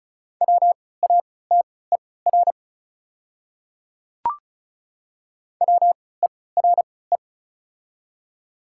Morse code, heard as a keyed tone, about 36 wpm, spaced as for 20 wpm, 700 Hz. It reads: WATER WERE
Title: 3x Character Spacing - Top 100 Words - 35wpm